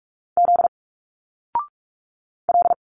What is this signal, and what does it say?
Morse code, heard as a keyed tone, about 46 wpm, 700 Hz. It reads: Z L
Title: Single Letters 45wpm